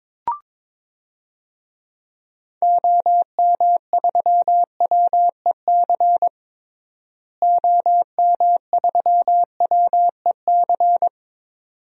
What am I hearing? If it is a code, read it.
Morse code, heard as a keyed tone, about 22 wpm, 700 Hz. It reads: OM3WEC OM3WEC
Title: DXCC-100 Call Signs - 6 Characters - Repeated 2x - 22wpm